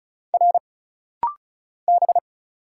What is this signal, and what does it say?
Morse code, heard as a keyed tone, about 34 wpm, 700 Hz. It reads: R B
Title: Single Letters - Rapid-Fire 35wpm